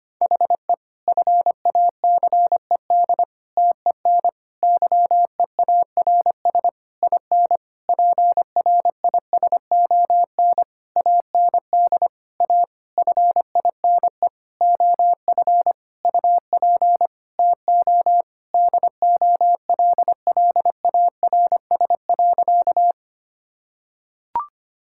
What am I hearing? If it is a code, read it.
Morse code, heard as a keyed tone, about 25 wpm, 700 Hz. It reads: HE FACED TEN YEARS IN PRISON AND A FINE OF UP TO DOLLARS.
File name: practice / 25wpm / Sentences with 9-Word Phrases - 25wpm